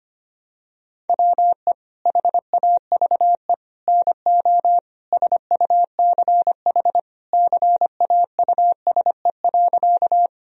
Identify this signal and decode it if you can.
Morse code, heard as a keyed tone, about 25 wpm, 700 Hz. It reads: WE HAVE NO SUCH CAUSE.
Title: Sentences from Top 1000 Words - Review Part 1 - 25wpm